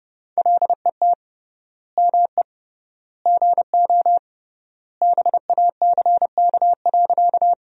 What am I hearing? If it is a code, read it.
Morse code, heard as a keyed tone, about 32 wpm, 700 Hz. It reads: LET ME GO BACK.
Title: 3x Word-Spacing - Sentences from Top 300 Words - 30wpm